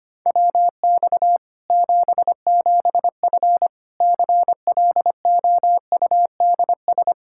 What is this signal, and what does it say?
Morse code, heard as a keyed tone, about 25 wpm, 700 Hz. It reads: WX 77F CLOUDS
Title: Practice QSO - Exchange 2 - Info - 25wpm